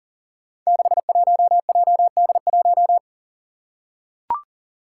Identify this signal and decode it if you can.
Morse code, heard as a keyed tone, about 40 wpm, 700 Hz. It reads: 61JD1
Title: Five Challenging Letter-Number Combinations 40wpm